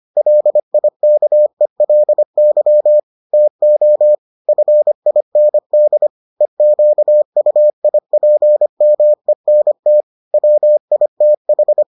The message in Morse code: LIKELY TO FIND EQUIPMENT WITH

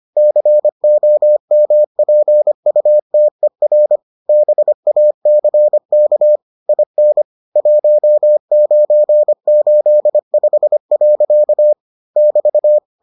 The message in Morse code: COMPUTER BACK IN 1985. <BT>

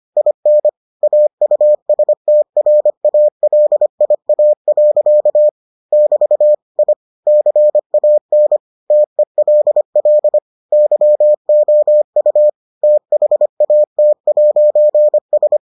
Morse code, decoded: IN AUSTRALIA. <BT> I CAN TELL YOU THAT'S

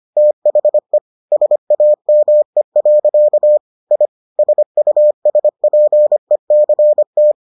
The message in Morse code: THE SAME. I SUSPECT

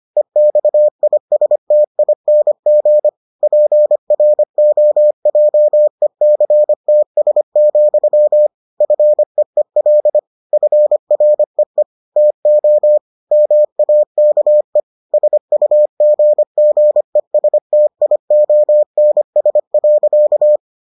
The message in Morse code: EXISTING PROJECTS, FEEL FREE TO MAKE SUGGESTIONS.

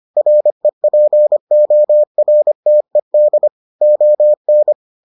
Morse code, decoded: REPORTED ON

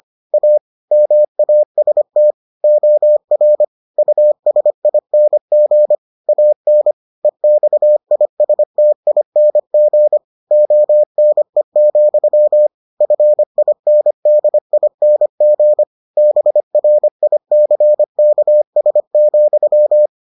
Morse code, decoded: A MAST OR USING AN EXISTING ONE, FINDING BRICKS,

